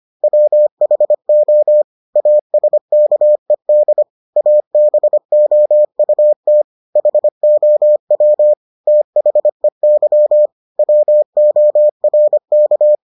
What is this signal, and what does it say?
Morse code, heard as a keyed tone, about 25 wpm, 600 Hz. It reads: WHO ASKED ABOUT HOW THEY WORK